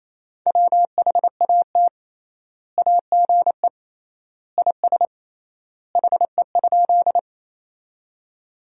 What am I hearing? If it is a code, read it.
Morse code, heard as a keyed tone, about 28 wpm, 700 Hz. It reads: WHAT AGE IS HE?
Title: 3x Word-Spacing - Sentences from Top 500 Words - 28wpm